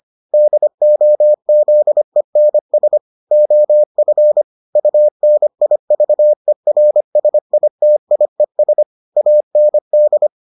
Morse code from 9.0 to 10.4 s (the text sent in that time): AND